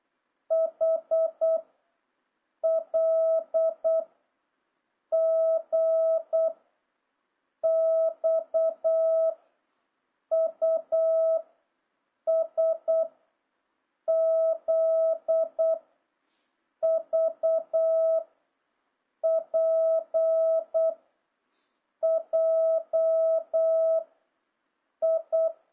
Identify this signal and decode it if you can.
Morse code, heard as a keyed tone, about 8 wpm, 650 Hz. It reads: H L G X U S Z V P J I
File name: jh8cbh / 2026cw